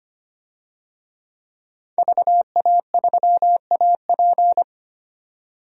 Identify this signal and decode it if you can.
Morse code, heard as a keyed tone, about 25 wpm, 700 Hz. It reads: VA3AP
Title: DXCC-100 Call Signs - 5 Characters - 25wpm